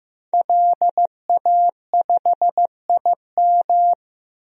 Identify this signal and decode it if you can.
Morse code, heard as a keyed tone, about 16 wpm, 700 Hz. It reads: LA5IM
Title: DXCC-100 Call Signs - 5 Characters - 15wpm